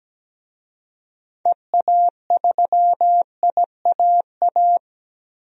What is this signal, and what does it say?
Morse code, heard as a keyed tone, about 17 wpm, 700 Hz. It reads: EA3IAA
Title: DXCC-100 Call Signs - 6 Characters - 17wpm